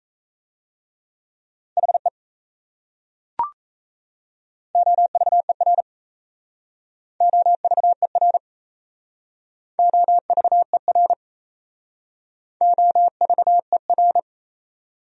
Morse code, heard as a keyed tone, about 37 wpm, 700 Hz. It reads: HE OVER OVER OVER OVER